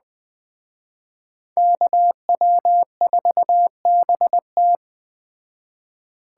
Morse code, heard as a keyed tone, about 20 wpm, 700 Hz. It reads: KW4BT
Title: Speed Racing - USA 2x2 Call Signs 17wpm